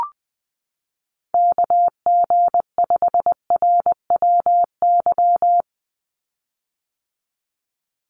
Morse code, heard as a keyed tone, about 20 wpm, 700 Hz. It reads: KG5RWY
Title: USA Call Signs 20wpm